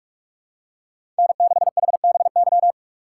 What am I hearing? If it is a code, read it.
Morse code, heard as a keyed tone, about 43 wpm, 700 Hz. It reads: N6HBY
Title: USA Call Signs - 6x Character Spacing - 45wpm